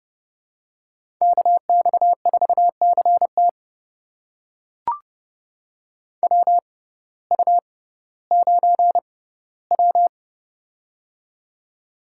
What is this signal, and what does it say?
Morse code, heard as a keyed tone, about 29 wpm, spaced as for 16 wpm, 700 Hz. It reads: KX4CT WU9W
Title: USA Call Signs - 6x Character Spacing - 30wpm